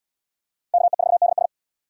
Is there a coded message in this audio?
Yes: Morse code from 0.7 to 1.5 s, heard as a keyed tone, about 69 wpm, 700 Hz. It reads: KE4DS